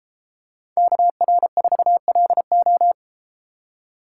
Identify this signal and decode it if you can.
Morse code, heard as a keyed tone, about 33 wpm, 700 Hz. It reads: KR4LO